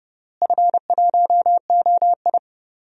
Morse code, heard as a keyed tone, about 32 wpm, 700 Hz. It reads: F1OI
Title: DXCC-100 Call Signs - 4 Characters - Repeated 2x - 30wpm